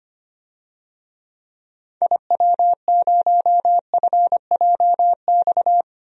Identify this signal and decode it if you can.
Morse code, heard as a keyed tone, about 25 wpm, 700 Hz. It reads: IW0FJX
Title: DXCC-100 Call Signs - 6 Characters - 25wpm